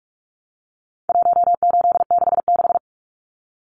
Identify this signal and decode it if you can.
Morse code, heard as a keyed tone, about 44 wpm, 700 Hz. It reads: 1866
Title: Numbers - 4-Digit 45wpm